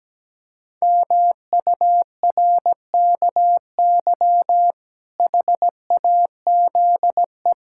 Morse code, heard as a keyed tone, about 17 wpm, 700 Hz. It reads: MURKY HAZE
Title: Pangram Challenge v2 - 17wpm